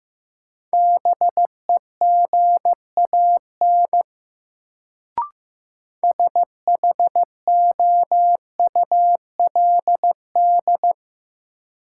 Morse code, heard as a keyed tone, about 15 wpm, 700 Hz. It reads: BEGAN SHOULD
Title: Sentences from Top 300 Words - New Word Drill - 15wpm